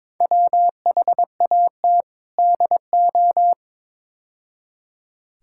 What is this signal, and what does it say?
Morse code, heard as a keyed tone, about 22 wpm, 700 Hz. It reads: WHAT DO
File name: practice / 22wpm / Build A Phrase - 5-Word Phrases - 22wpm